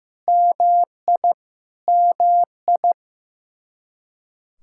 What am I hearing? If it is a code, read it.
Morse code, heard as a keyed tone, about 15 wpm, 700 Hz. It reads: MI MI
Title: Practice QSO - Exchange 1 - Beginning - 15wpm